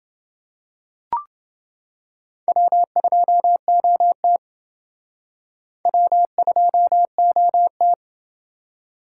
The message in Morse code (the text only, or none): W2OT W2OT